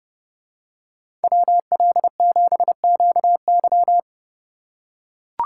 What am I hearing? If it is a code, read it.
Morse code, heard as a keyed tone, about 28 wpm, 700 Hz. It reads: WL7QY